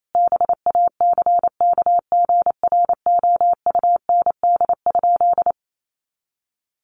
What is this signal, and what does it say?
Morse code, heard as a keyed tone, about 28 wpm, 700 Hz. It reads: BACKGROUND?
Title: Sentences from Top 1300 Words 28wpm